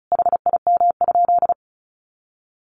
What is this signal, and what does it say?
Morse code, heard as a keyed tone, about 34 wpm, 700 Hz. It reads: HIM?
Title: Sentences from Top 100 Words 35wpm